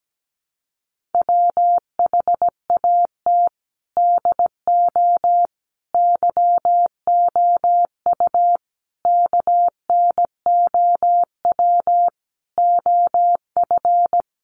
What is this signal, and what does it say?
Morse code, heard as a keyed tone, about 17 wpm, 700 Hz. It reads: WHAT DO YOU KNOW OF
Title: Sentences from Top 100 Words 17wpm